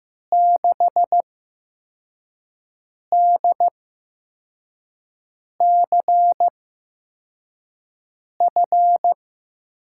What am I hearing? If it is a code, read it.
Morse code, heard as a keyed tone, about 15 wpm, 700 Hz. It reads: 6 D C F
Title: USA Call Signs - 5 Characters - 8x Character Spacing - 15wpm